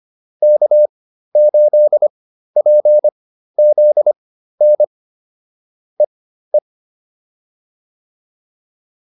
Morse code, heard as a keyed tone, about 25 wpm, spaced as for 13 wpm, 600 Hz. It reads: K8PZN EE